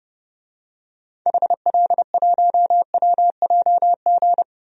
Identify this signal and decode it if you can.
Morse code, heard as a keyed tone, about 32 wpm, 700 Hz. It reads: HL1WJG
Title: DXCC-100 Call Signs - 6 Characters - Repeated 2x - 30wpm